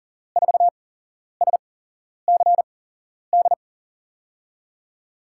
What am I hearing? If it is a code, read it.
Morse code, heard as a keyed tone, about 40 wpm, spaced as for 11 wpm, 700 Hz. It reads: 4SCD
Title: USA Call Signs - 8x Character Spacing - 40wpm